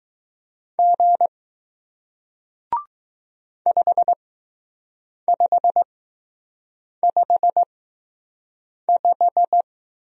Morse code, heard as a keyed tone, about 20 wpm, spaced as for 6 wpm, 700 Hz. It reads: G 5555